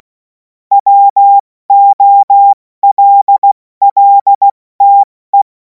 Morse code, heard as a keyed tone, about 16 wpm, spaced as for 14 wpm, 800 Hz. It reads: WOLLTE
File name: Morsekurs / Witze / Witz 31. März 2026 70 F80